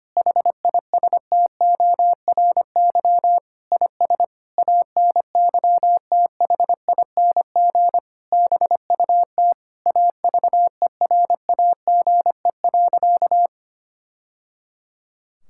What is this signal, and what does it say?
Morse code, heard as a keyed tone, about 25 wpm, 700 Hz. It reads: HISTORY IS ANYTHING BUT AVERAGE.
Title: Sentences from Top 1000 Words - Review Part 6 - 25wpm